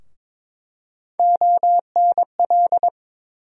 Morse code, heard as a keyed tone, about 21 wpm, 700 Hz. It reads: ONL
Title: USA Call Sign Postfixes - 3 Letters 22wpm